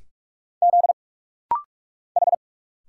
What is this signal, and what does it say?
Morse code, heard as a keyed tone, about 46 wpm, 700 Hz. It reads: Z H
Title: Single Letter-Number Alt - Rapid-Fire 45wpm